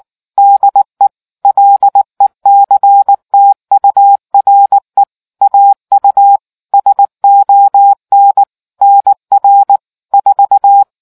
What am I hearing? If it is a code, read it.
Morse code, heard as a keyed tone, about 19 wpm, 800 Hz. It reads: DE LECTURE AU SON NR 4